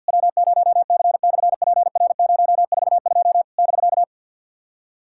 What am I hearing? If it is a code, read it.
Morse code, heard as a keyed tone, about 50 wpm, 700 Hz. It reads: W0C/PR042 <BK>
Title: SOTA QSO Practice 50wpm